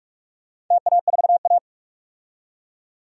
Morse code, heard as a keyed tone, about 46 wpm, 700 Hz. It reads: TA4A
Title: DXCC-100 Call Signs - 4 Characters - 45wpm